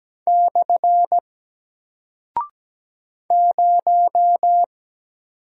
Morse code, heard as a keyed tone, about 17 wpm, 700 Hz. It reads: / 0
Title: Single Character (Letter:Number:Punctuation:Prosign) Alt 17wpm